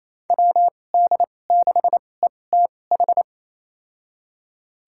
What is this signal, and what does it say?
Morse code, heard as a keyed tone, about 27 wpm, spaced as for 20 wpm, 700 Hz. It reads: WD6ETH
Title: USA Call Signs - 6 Characters - 2x Character Spacing - 28wpm